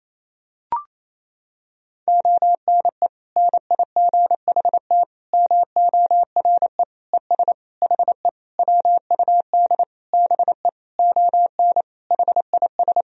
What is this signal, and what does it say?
Morse code, heard as a keyed tone, about 28 wpm, 700 Hz. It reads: ONE NIGHT MORE ES HE WUD BE ON HIS
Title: Sentences from Top 300 Words and Abbreviations 28wpm